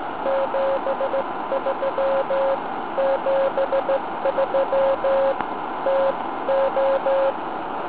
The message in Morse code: 73 73 TO